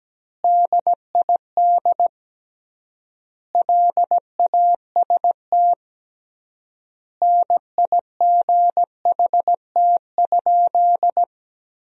Morse code, heard as a keyed tone, about 17 wpm, 700 Hz. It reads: DID LAST NIGHT?
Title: 3x Word-Spacing - Sentences from Top 300 Words - 17wpm